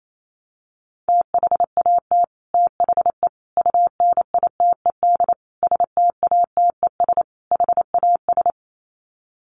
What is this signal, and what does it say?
Morse code, heard as a keyed tone, about 28 wpm, 700 Hz. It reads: THAT THE UNITED STATES HAS
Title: Sets of 5 Words Encore 28wpm